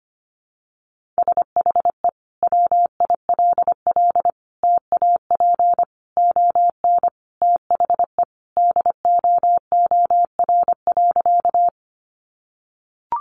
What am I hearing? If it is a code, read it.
Morse code, heard as a keyed tone, about 25 wpm, 700 Hz. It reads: SHE WILL TAP ON THE DOOR.
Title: Sentences from Top 1400 Words 25wpm